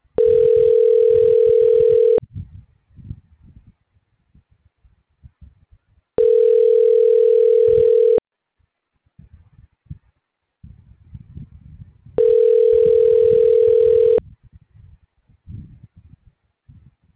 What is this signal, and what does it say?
Ringback tone (North America), 440 + 480 Hz.